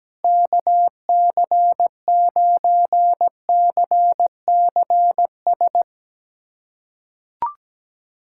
Morse code, heard as a keyed tone, about 17 wpm, 700 Hz. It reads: KC9CCS
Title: USA Callsign DE Callsign 17wpm